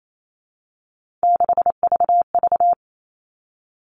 Morse code, heard as a keyed tone, about 28 wpm, 700 Hz. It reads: 6VV
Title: All Cut Numbers - 3-Digit 28wpm